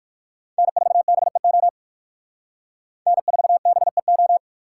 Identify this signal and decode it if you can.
Morse code, heard as a keyed tone, about 63 wpm, 700 Hz. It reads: N4BEO N4BEO